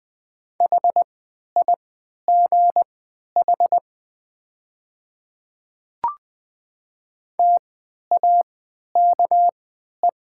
Morse code, heard as a keyed tone, about 20 wpm, spaced as for 11 wpm, 700 Hz. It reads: HIGH TAKE
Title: 3x Character Spacing - Top 200 Words - 20wpm